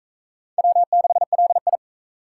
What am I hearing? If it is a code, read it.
Morse code, heard as a keyed tone, about 40 wpm, 700 Hz. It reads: W6LI